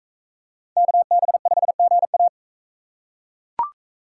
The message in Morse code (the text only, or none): KB5GA